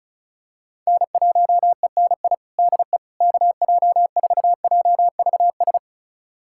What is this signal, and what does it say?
Morse code, heard as a keyed tone, about 36 wpm, 700 Hz. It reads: N1ENI DE KJ4JVS
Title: USA Callsign DE Callsign 35wpm